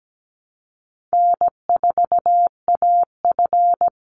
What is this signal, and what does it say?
Morse code, heard as a keyed tone, about 17 wpm, 700 Hz. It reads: N4AF
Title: Easy Call Signs Repeated 2x - 17wpm